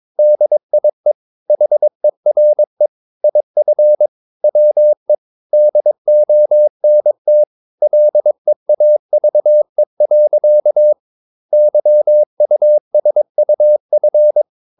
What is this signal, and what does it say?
Morse code, heard as a keyed tone, about 22 wpm, 600 Hz. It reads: DIE HERE IF WE DONT LEAVE. YUSUF
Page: Romanen Never af Ken Follet på engelsk.